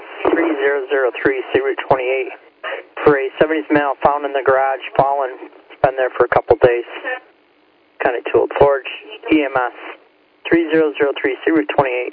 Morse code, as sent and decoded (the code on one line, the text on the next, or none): none